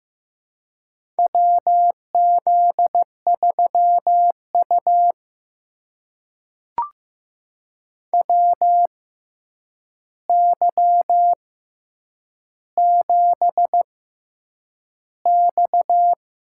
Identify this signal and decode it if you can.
Morse code, heard as a keyed tone, about 15 wpm, spaced as for 8 wpm, 700 Hz. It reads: WZ3U WY7X
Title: USA Call Signs - 4 Characters - 6x Character Spacing - 15wpm